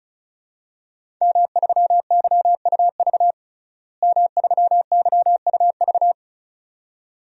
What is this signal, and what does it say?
Morse code, heard as a keyed tone, about 36 wpm, 700 Hz. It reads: M3YUV M3YUV
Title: DXCC-100 Call Signs - 5 Characters - Repeated 2x - 35wpm